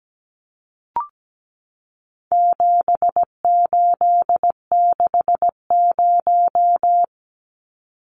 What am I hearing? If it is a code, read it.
Morse code, heard as a keyed tone, about 17 wpm, 700 Hz. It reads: 7860
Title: Numbers - 4-Digit 17wpm